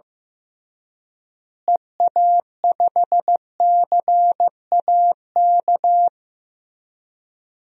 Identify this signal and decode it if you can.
Morse code, heard as a keyed tone, about 16 wpm, 700 Hz. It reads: EA5CAK
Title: DXCC-100 Call Signs - 6 Characters - Repeated 3x - 15wpm